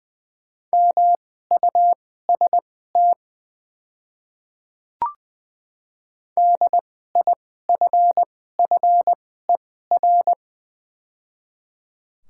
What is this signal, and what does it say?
Morse code, heard as a keyed tone, about 20 wpm, spaced as for 14 wpm, 700 Hz. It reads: MUST DIFFER
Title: 2x Character Spacing - Top 200 Words - 20wpm